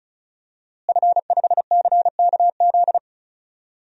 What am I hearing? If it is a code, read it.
Morse code, heard as a keyed tone, about 34 wpm, 700 Hz. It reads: F5CKZ
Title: DXCC-100 Call Signs - 5 Characters - Repeated 2x - 35wpm